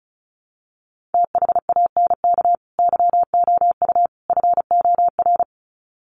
Morse code, heard as a keyed tone, about 34 wpm, 700 Hz. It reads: THANK YOU FOR
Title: Sets of 3 Words 35wpm